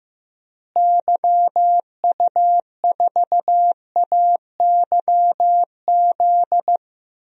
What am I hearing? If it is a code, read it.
Morse code, heard as a keyed tone, about 15 wpm, 700 Hz. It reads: YU4AYZ
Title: DXCC-100 Call Signs - 6 Characters - Repeated 3x - 15wpm